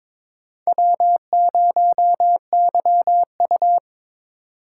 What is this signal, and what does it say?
Morse code, heard as a keyed tone, about 21 wpm, 700 Hz. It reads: W0YU
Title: USA Call Signs - 4 Characters - 4x Character Spacing - 22wpm